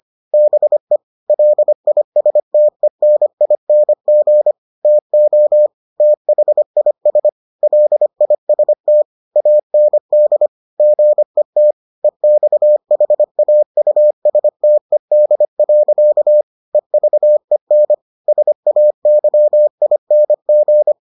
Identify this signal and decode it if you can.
Morse code, heard as a keyed tone, about 25 wpm, 600 Hz. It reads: BE LISTENING TO THIS LIST AND GET EXHAUSTED. EVEN SAYING